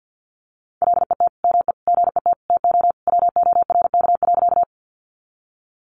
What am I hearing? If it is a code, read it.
Morse code, heard as a keyed tone, about 49 wpm, 700 Hz. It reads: LET ME GET TO WORK.
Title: Sentences from Top 300 Words 50wpm